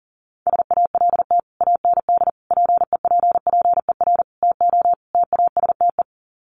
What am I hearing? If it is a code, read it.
Morse code, heard as a keyed tone, about 40 wpm, 700 Hz. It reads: SALT AND PEPPER TO TASTE
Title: Sets of 5 Words 40wpm